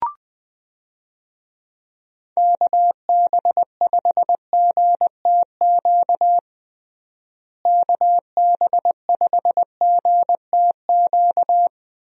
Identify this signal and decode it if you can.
Morse code, heard as a keyed tone, about 21 wpm, 700 Hz. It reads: KB5GTQ KB5GTQ
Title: USA Call Signs - 6 Characters - Repeated 2x - 20wpm